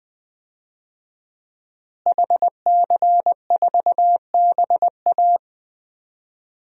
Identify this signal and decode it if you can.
Morse code, heard as a keyed tone, about 21 wpm, 700 Hz. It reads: HC4BA
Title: DXCC-100 Call Signs - 5 Characters - Repeated 3x - 20wpm